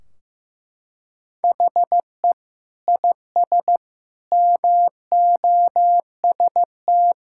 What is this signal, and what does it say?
Morse code, heard as a keyed tone, about 15 wpm, 700 Hz. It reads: HE IS MOST
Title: Sentences from Top 1600 Words 15wpm